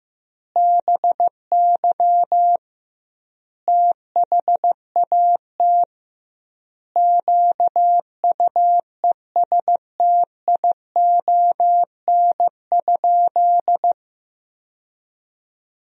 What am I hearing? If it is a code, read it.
Morse code, heard as a keyed tone, about 16 wpm, 700 Hz. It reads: BY THAT QUESTION?
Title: 2x Word-Spacing - Sentences from Top 400 Words - 15wpm